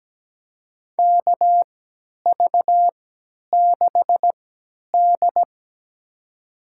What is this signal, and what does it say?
Morse code, heard as a keyed tone, about 17 wpm, spaced as for 10 wpm, 700 Hz. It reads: KV6D
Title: USA Call Signs - 3x Character Spacing - 17wpm